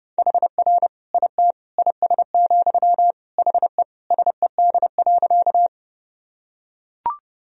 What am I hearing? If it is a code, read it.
Morse code, heard as a keyed tone, about 32 wpm, 700 Hz. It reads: HR IT IS, HE SED.
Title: Sentences from Top 200 Words and Abbreviations 30wpm